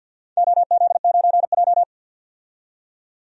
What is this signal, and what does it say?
Morse code, heard as a keyed tone, about 53 wpm, 700 Hz. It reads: OZ9J
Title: DXCC-100 Call Signs - 4 Characters - 50wpm